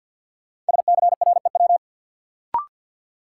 Morse code, heard as a keyed tone, about 53 wpm, 700 Hz. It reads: SCREW
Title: Sentences from Top 1600 Words - New Word Drill Alt - 50wpm